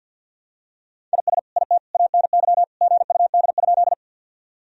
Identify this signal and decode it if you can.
Morse code, heard as a keyed tone, about 52 wpm, 700 Hz. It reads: IS IT ANY GUD?
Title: Sentences from Top 200 Words and Abbreviations 50wpm